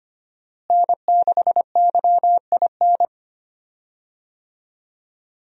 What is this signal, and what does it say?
Morse code, heard as a keyed tone, about 25 wpm, 700 Hz. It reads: N6YIN
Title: USA Call Signs - 5 Characters - Repeated 2x - 25wpm